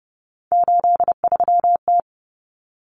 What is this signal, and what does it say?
Morse code, heard as a keyed tone, about 30 wpm, 700 Hz. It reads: 83T
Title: Common Cut Numbers - 3-Digit 30wpm